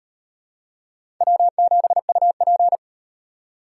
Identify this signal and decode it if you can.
Morse code, heard as a keyed tone, about 35 wpm, 700 Hz. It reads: W7UP